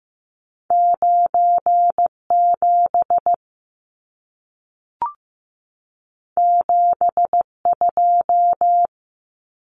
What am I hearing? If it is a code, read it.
Morse code, heard as a keyed tone, about 15 wpm, 700 Hz. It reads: 97 72
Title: Numbers - 2-Digit 15wpm